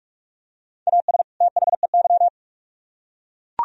Morse code, heard as a keyed tone, about 39 wpm, 700 Hz. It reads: AS THEY